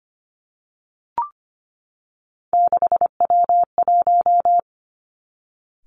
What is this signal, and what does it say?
Morse code, heard as a keyed tone, about 25 wpm, 700 Hz. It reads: 6W1
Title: Three Challenging Letter-Number Combinations 25wpm